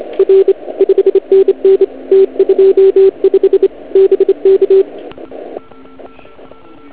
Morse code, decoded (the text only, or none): R 5NN T25 BK